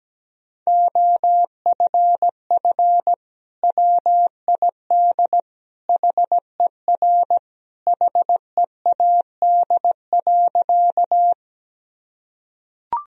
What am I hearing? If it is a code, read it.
Morse code, heard as a keyed tone, about 17 wpm, 700 Hz. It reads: OFF WID HER HEAD.
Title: Sentences from Top 300 Words and Abbreviations 17wpm